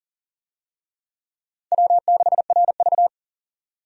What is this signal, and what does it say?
Morse code, heard as a keyed tone, about 40 wpm, 700 Hz. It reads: W6RV
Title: USA Call Signs - 4 Characters - 40wpm